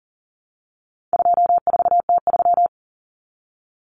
Morse code, heard as a keyed tone, about 40 wpm, 700 Hz. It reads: 24T3
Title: Common Cut Numbers - 4-Digit 40wpm